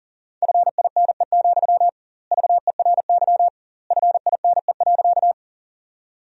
Morse code, heard as a keyed tone, about 37 wpm, 700 Hz. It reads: FINE, VERY FINE.
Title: 2x Word-Spacing - Sentences from Top 500 Words - 40wpm